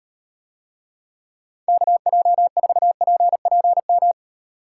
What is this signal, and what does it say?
Morse code, heard as a keyed tone, about 38 wpm, 700 Hz. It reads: KJ4PPM